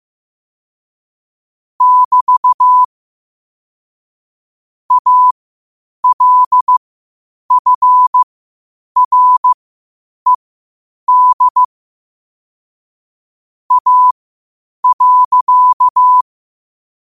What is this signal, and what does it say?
Morse code, heard as a keyed tone, about 15 wpm, spaced as for 8 wpm, 1000 Hz. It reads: <BT> ALFRED A.
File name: qotdfiles / 8wpm